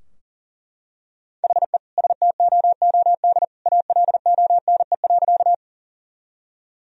Morse code, heard as a keyed tone, about 40 wpm, 700 Hz. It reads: HE STOOD ALONE.
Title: Sentences from Top 1000 Words - Review Part 2 - 40wpm